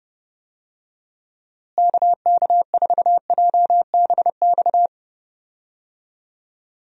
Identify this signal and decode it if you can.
Morse code, heard as a keyed tone, about 31 wpm, 700 Hz. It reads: KK4JBX
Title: USA Call Signs - 6 Characters - Repeated 2x - 30wpm